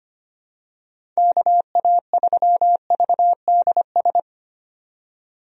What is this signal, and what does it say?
Morse code, heard as a keyed tone, about 25 wpm, 700 Hz. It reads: KA3VDS
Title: USA Call Signs - 6 Characters - 25wpm